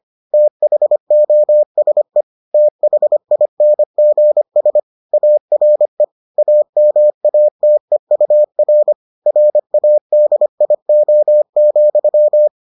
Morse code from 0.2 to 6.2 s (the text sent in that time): THOSE THINGS ARE